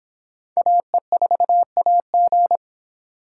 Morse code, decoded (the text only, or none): AE4AG